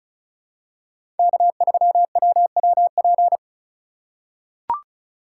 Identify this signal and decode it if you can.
Morse code, heard as a keyed tone, about 34 wpm, 700 Hz. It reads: K3WWP
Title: K1USN SST QSO Practice - 35wpm